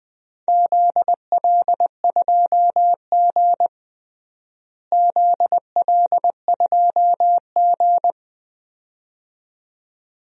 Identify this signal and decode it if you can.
Morse code, heard as a keyed tone, about 20 wpm, 700 Hz. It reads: ZL2G ZL2G
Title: DXCC-100 Call Signs - 4 Characters - Repeated 2x - 20wpm